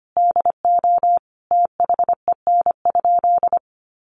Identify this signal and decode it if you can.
Morse code, heard as a keyed tone, about 25 wpm, 700 Hz. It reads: DO THEN?
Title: Sentences from Top 100 Words 25wpm